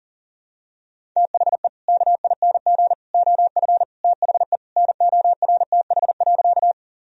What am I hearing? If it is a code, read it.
Morse code, heard as a keyed tone, about 37 wpm, 700 Hz. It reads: THE KING OF THE NORTH.
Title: Sentences from Top 1000 Words - Review Part 1 - 40wpm